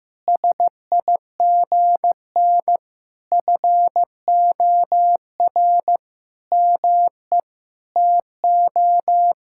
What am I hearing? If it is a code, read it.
Morse code, heard as a keyed tone, about 15 wpm, 700 Hz. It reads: SIGN FOR ME TO
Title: Sentences from Top 1000 Words - Review Part 6 - 15wpm